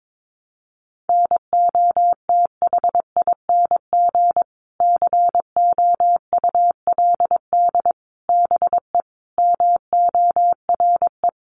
Morse code, heard as a keyed tone, about 22 wpm, 700 Hz. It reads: NOTHING COULD BE MORE
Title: Sentences from Top 500 Words 22wpm